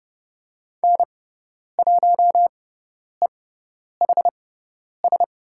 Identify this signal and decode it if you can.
Morse code, heard as a keyed tone, about 30 wpm, spaced as for 10 wpm, 700 Hz. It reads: N1EHS
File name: Lesson 18g - Letter m - Call Signs 30wpm